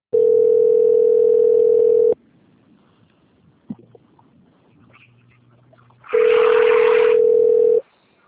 Ringback tone (North America), 440 + 480 Hz.
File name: out- - - - - .66492